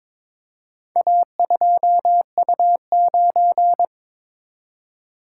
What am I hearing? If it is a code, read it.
Morse code, heard as a keyed tone, about 22 wpm, 700 Hz. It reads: A2U9
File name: practice / 22wpm / Four Letter-Number Combinations - 22wpm